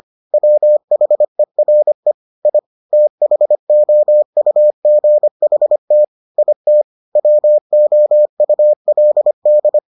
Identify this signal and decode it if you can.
Morse code, heard as a keyed tone, about 25 wpm, 600 Hz. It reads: WHERE I THOUGHT IT WOULD